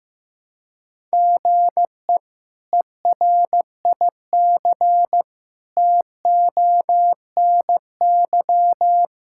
Morse code, heard as a keyed tone, about 16 wpm, 700 Hz. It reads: GE ERIC TONY
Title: K1USN SST QSO Practice - 15wpm